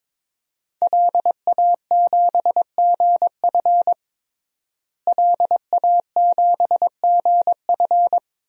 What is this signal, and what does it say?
Morse code, heard as a keyed tone, about 21 wpm, 700 Hz. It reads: LA7GF LA7GF
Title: DXCC-100 Call Signs - 5 Characters - Repeated 3x - 22wpm